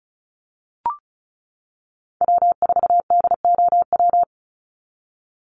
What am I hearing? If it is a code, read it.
Morse code, heard as a keyed tone, about 34 wpm, 700 Hz. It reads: W4DOW
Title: NA SOTA Chaser Callsigns 35wpm